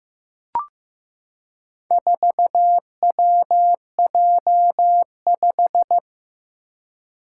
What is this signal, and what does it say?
Morse code, heard as a keyed tone, about 15 wpm, 700 Hz. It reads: 4WJ5
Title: Four Challenging Letter-Number Combinations 15wpm